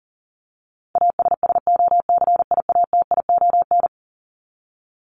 Morse code, heard as a keyed tone, about 40 wpm, 700 Hz. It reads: ASSOCIATION
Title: Top 1001-1250 Words 40wpm